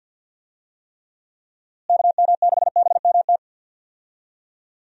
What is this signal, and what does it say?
Morse code, heard as a keyed tone, about 46 wpm, 700 Hz. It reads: KM6BMT
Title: USA Call Signs - 6 Characters - Repeated 3x - 50wpm